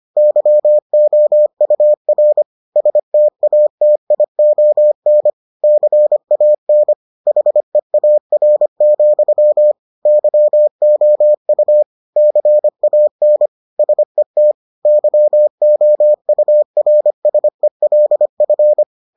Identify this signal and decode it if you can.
Morse code, heard as a keyed tone, about 25 wpm, 600 Hz. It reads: YOUR STATION CAN HEAR, YOU CAN SET YOURSELF